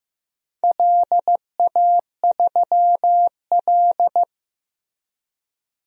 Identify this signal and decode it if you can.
Morse code, heard as a keyed tone, about 15 wpm, 700 Hz. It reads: LA3L
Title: DXCC-100 Call Signs - 4 Characters - 15wpm